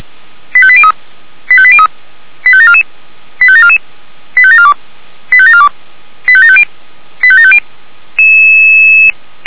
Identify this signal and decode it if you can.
Five-tone selective call (ZVEI), heard as quick runs of five tones, tones of about 70 ms. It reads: 85803 85803 85744 85744 85742 85742 85750 85750